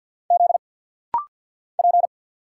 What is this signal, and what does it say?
Morse code, heard as a keyed tone, about 48 wpm, 700 Hz. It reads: Z P
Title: Single Letter-Number - Rapid-Fire 50wpm